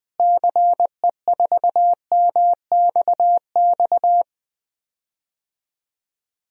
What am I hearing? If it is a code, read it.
Morse code, heard as a keyed tone, about 20 wpm, 700 Hz. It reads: CE4MXX
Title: DXCC-100 Call Signs - 6 Characters - Repeated 2x - 20wpm